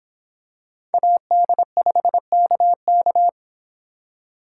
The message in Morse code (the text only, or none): AD5KK